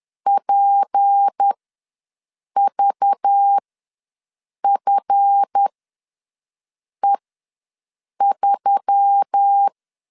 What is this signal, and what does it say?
Morse code, heard as a keyed tone, about 10 wpm, spaced as for 6 wpm, 800 Hz. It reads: PVFE3
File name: LEZIONI / 8 YJ45 / 8 YJ ws 10wpm 4043